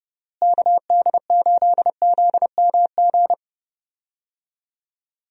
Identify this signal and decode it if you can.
Morse code, heard as a keyed tone, about 30 wpm, 700 Hz. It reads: KD8ZMG
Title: USA Call Signs - 6 Characters - Repeated 2x - 30wpm